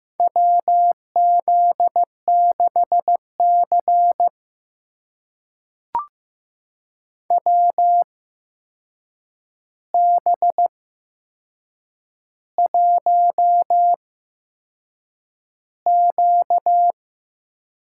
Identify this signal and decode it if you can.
Morse code, heard as a keyed tone, about 15 wpm, spaced as for 6 wpm, 700 Hz. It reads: WZ6C WB1Q
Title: USA Call Signs - 4 Characters - 8x Character Spacing - 15wpm